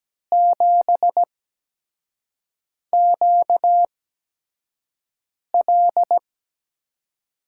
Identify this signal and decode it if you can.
Morse code, heard as a keyed tone, about 17 wpm, spaced as for 5 wpm, 700 Hz. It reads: 7QL
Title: USA Call Signs - 5 Characters - 8x Character Spacing - 17wpm